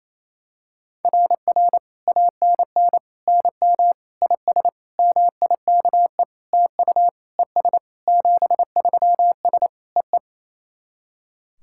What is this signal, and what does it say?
Morse code, heard as a keyed tone, about 28 wpm, 700 Hz. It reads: RR ANN NM IS MIKE TU ES 73S EE
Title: SOTA QSO Practice 28wpm